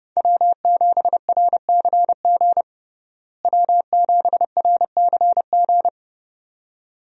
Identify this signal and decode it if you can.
Morse code, heard as a keyed tone, about 32 wpm, 700 Hz. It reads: W7RCG W7RCG
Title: USA Call Signs - 5 Characters - Repeated 2x - 30wpm